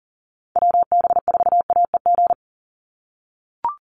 Morse code, heard as a keyed tone, about 40 wpm, 700 Hz. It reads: WB4AEG
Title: Call Signs - 40wpm